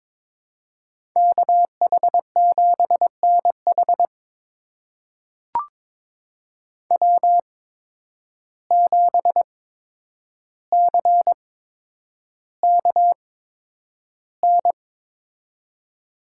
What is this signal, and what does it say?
Morse code, heard as a keyed tone, about 22 wpm, spaced as for 9 wpm, 700 Hz. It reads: KH7NH W7CKN
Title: USA Call Signs - 5 Characters - 8x Character Spacing - 22wpm